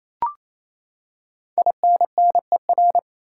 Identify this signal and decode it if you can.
Morse code, heard as a keyed tone, about 28 wpm, 700 Hz. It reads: INNER
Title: Five Letter Words - 28wpm